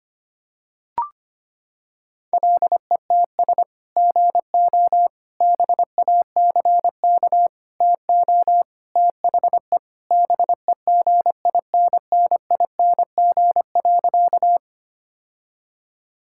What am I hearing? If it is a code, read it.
Morse code, heard as a keyed tone, about 25 wpm, 700 Hz. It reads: LETS GO BACK TO THE BEGINNING.
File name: practice / 25wpm / Sentences with 3-Word Phrases - 25wpm